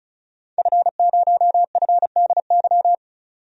Morse code, heard as a keyed tone, about 34 wpm, 700 Hz. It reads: F0FDY
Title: DXCC-100 Call Signs - 5 Characters - 35wpm